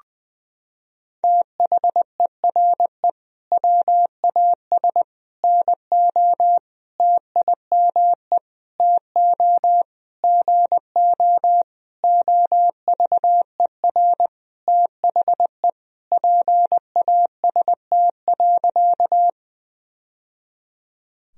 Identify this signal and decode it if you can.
Morse code, heard as a keyed tone, about 20 wpm, 700 Hz. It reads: THERE WAS NO TIME TO GO OVER THE PAST.
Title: Sentences from Top 1000 Words - Review Part 3 - 20wpm